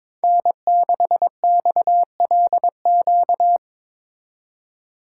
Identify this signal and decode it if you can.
Morse code, heard as a keyed tone, about 24 wpm, 700 Hz. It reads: N6XLQ
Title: Speed Racing - USA 1x3 Call Signs 22wpm